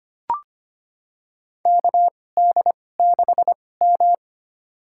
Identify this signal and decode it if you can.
Morse code, heard as a keyed tone, about 25 wpm, spaced as for 18 wpm, 700 Hz. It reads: KD6M
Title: USA Call Signs - 4 Characters - 2x Character Spacing - 25wpm